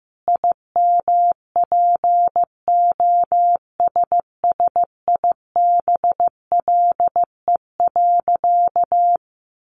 Morse code, heard as a keyed tone, about 14 wpm, 700 Hz. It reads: IMPOSSIBLE.
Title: Sentences from Top 1100 Words 15wpm